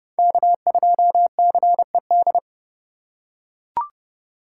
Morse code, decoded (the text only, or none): K2CED